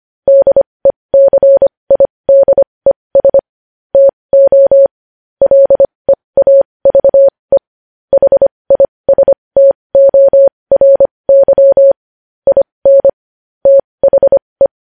Morse code, heard as a keyed tone, about 25 wpm, spaced as for 20 wpm, 550 Hz. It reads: DECIDES TO LEAVE HISTORY IN THE
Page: news-headlines-in-morse-code-at-20-wpm